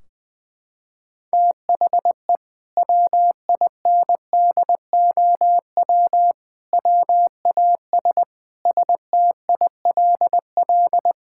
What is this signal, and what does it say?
Morse code, heard as a keyed tone, about 20 wpm, 700 Hz. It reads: THE WINDOW WAS STILL
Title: Sentences from Top 1600 Words 20wpm